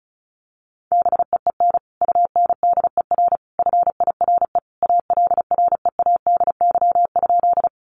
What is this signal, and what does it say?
Morse code, heard as a keyed tone, about 34 wpm, 700 Hz. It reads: BEEN UNDER FIRE ALREADY?
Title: Sentences from Top 700 Words 35wpm